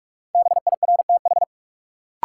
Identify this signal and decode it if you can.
Morse code, heard as a keyed tone, about 46 wpm, 700 Hz. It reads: BIRTH
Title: Sentences from Top 800 Words - New Word Drill - 45wpm